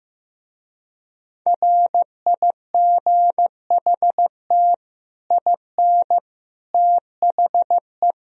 Morse code, heard as a keyed tone, about 15 wpm, 700 Hz. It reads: RIGHT IN THE
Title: Speed-Racing - Sets of 5 Words 15wpm